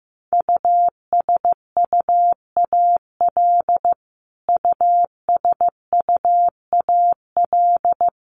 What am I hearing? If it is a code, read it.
Morse code, heard as a keyed tone, about 15 wpm, 700 Hz. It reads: USUAL USUAL
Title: Top 500 Words Repeated 15wpm